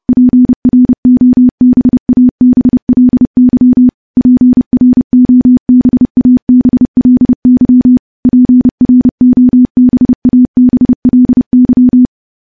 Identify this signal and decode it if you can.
Morse code, heard as a keyed tone, about 30 wpm, 250 Hz. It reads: PROBABLY PROBABLY PROBABLY
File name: morse / morse-training-30WPM